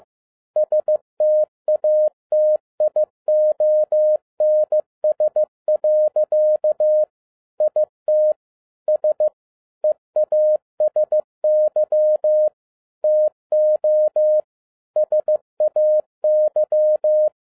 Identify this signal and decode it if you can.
Morse code, heard as a keyed tone, about 15 wpm, 610 Hz. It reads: STATIONS. IT S EASY TO SAY
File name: 2011-07-mp3 / n47 JOHN K3TN 15wpm